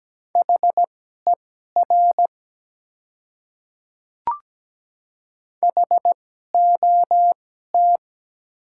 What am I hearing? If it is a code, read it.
Morse code, heard as a keyed tone, about 17 wpm, spaced as for 12 wpm, 700 Hz. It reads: HER HOT
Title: 2x Character Spacing - Top 100 Words - 17wpm